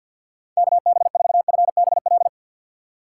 Morse code, heard as a keyed tone, about 47 wpm, 700 Hz. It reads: KB4FBL
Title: USA Call Signs - 6 Characters - 6x Character Spacing - 50wpm